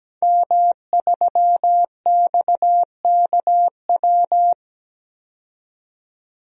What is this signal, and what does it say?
Morse code, heard as a keyed tone, about 17 wpm, 700 Hz. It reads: M3XKW
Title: DXCC-100 Call Signs - 5 Characters - 17wpm